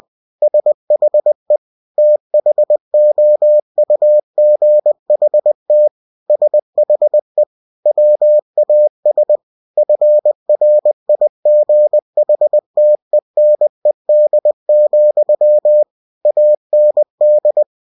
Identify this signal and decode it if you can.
Morse code, heard as a keyed tone, about 20 wpm, 600 Hz. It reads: SHE THOUGHT SHE WAS FRIGHTENED, AND
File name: cw-books / winnie-the-pooh / winnie-20wpm-page-0071